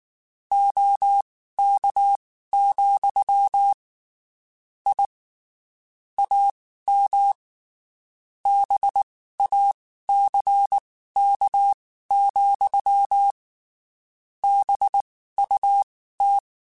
Morse code, heard as a keyed tone, about 19 wpm, spaced as for 14 wpm, 800 Hz. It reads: OK, I AM BACK, BUT
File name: (spaced) / QSOs Spaced-mp3s / kc7sac @18 sp